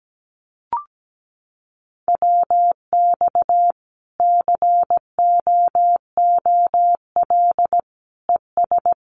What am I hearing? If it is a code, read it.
Morse code, heard as a keyed tone, about 17 wpm, 700 Hz. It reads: WX COOL ES
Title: Common QSO Phrases 17wpm